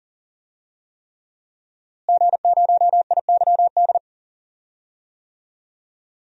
Morse code, heard as a keyed tone, about 40 wpm, 700 Hz. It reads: G0IYD
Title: DXCC-100 Call Signs - 5 Characters - Repeated 3x - 40wpm